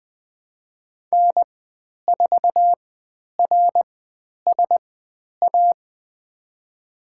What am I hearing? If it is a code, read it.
Morse code, heard as a keyed tone, about 20 wpm, spaced as for 10 wpm, 700 Hz. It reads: N4RSA
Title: 102 Lesson 17g - Letter u - Call Signs 20wpm